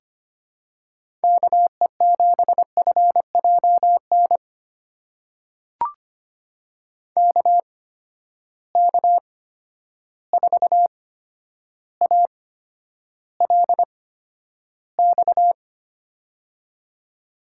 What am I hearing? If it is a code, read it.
Morse code, heard as a keyed tone, about 25 wpm, spaced as for 11 wpm, 700 Hz. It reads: KE7FJN KK4ALX
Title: USA Call Signs - 6 Characters - 8x Character Spacing - 25wpm